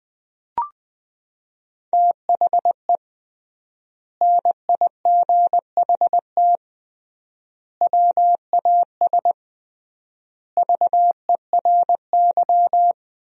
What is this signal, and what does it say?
Morse code, heard as a keyed tone, about 20 wpm, 700 Hz. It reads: THE NIGHT WAS VERY
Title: 3x Word-Spacing - Sentences from Top 500 Words - 20wpm